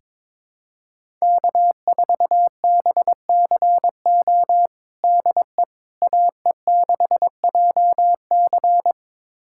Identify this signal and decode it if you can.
Morse code, heard as a keyed tone, about 22 wpm, 700 Hz. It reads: K4BCO DE AE6JC
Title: USA Callsign DE Callsign - 5 Characters - 22wpm